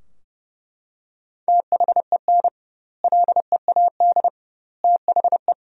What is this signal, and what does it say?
Morse code, heard as a keyed tone, about 32 wpm, 700 Hz. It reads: THEN LEAD THE
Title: 2x Word-Spacing - Sentences from Top 500 Words - 30wpm